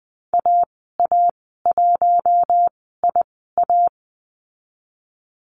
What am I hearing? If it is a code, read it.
Morse code, heard as a keyed tone, about 20 wpm, spaced as for 14 wpm, 700 Hz. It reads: AA1IA
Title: 042 Lesson 8g - Number 1 - Call Signs 20wpm